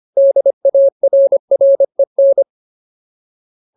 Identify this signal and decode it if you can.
Morse code, heard as a keyed tone, about 25 wpm, 550 Hz. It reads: DARREN
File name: Ultimate Word List / 25WPM Ultimate